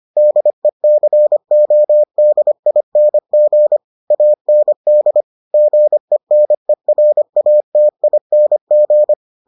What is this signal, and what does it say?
Morse code, heard as a keyed tone, about 25 wpm, 600 Hz. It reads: DECODING AND GENERATING